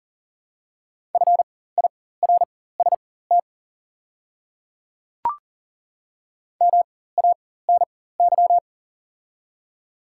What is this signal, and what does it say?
Morse code, heard as a keyed tone, about 40 wpm, spaced as for 19 wpm, 700 Hz. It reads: FIRST MANY
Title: 4x Character Spacing - Top 200 Words - 40wpm